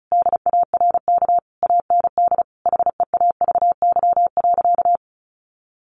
Morse code, heard as a keyed tone, about 34 wpm, 700 Hz. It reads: DARK AND HEAVY.
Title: Sentences from Top 1400 Words 35wpm